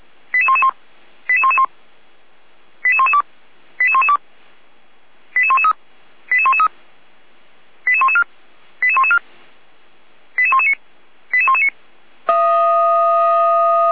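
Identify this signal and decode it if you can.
Five-tone selective call (ZVEI), heard as quick runs of five tones, tones of about 70 ms. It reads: 88111 88111 88112 88112 88113 88113 88114 88114 88119 88119
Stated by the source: ZVEI five-tone selective call 88497, received by SDR.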